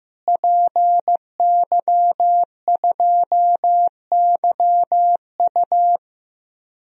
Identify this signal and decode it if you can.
Morse code, heard as a keyed tone, about 15 wpm, 700 Hz. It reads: PY2YU
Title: DXCC-100 Call Signs - 5 Characters - Repeated 3x - 15wpm